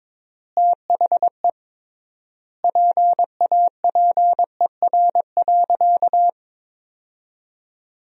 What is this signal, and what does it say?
Morse code, heard as a keyed tone, about 21 wpm, 700 Hz. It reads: THE PAPER.
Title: 3x Word-Spacing - Sentences from Top 300 Words - 22wpm